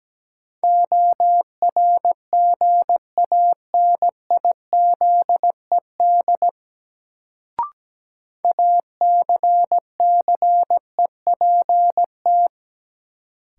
Morse code, heard as a keyed tone, about 17 wpm, 700 Hz. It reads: ORGANIZED ACCEPT
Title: Sentences from Top 1300 Words - New Word Drill - 17wpm